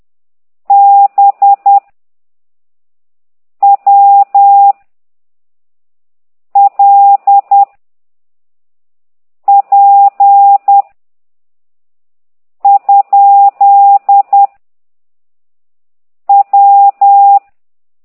Morse code, heard as a keyed tone, about 10 wpm, 800 Hz. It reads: B W L P ? W
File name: ct2idw / exerc9